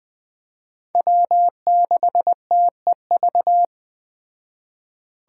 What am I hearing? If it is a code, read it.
Morse code, heard as a keyed tone, about 19 wpm, 700 Hz. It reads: W6TEV